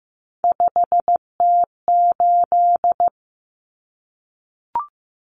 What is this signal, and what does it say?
Morse code, heard as a keyed tone, about 15 wpm, 700 Hz. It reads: 5T8
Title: Common Cut Numbers - 3-Digit 15wpm